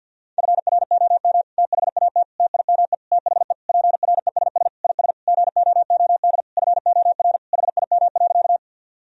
Morse code, heard as a keyed tone, about 53 wpm, 700 Hz. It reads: FROM THAT TIME THE PRESS IS GOOD FOR HIM.
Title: Sentences from Top 1000 Words - Review Part 1 - 50wpm